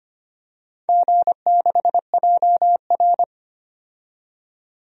Morse code, heard as a keyed tone, about 25 wpm, 700 Hz. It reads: G6JR
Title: DXCC-100 Call Signs - 4 Characters - 25wpm